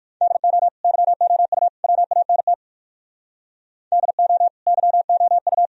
Morse code, heard as a keyed tone, about 61 wpm, 700 Hz. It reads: DO YOU WANT DO YOU